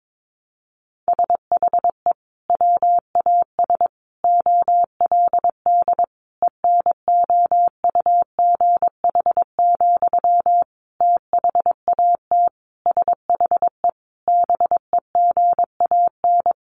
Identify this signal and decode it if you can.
Morse code, heard as a keyed tone, about 22 wpm, 700 Hz. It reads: SHE WAS OLD ENOUGH, THAT SHE BEGAN
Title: Sentences from Top 800 Words 22wpm